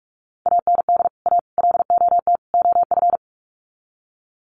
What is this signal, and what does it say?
Morse code, heard as a keyed tone, about 44 wpm, 700 Hz. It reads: AND A LOT OF
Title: Sets of 4 Words 45wpm